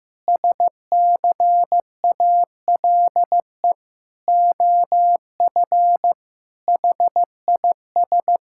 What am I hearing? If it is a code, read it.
Morse code, heard as a keyed tone, about 15 wpm, 700 Hz. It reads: SCALE OF HIS
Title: Sentences from Top 1000 Words - Review Part 5 - 15wpm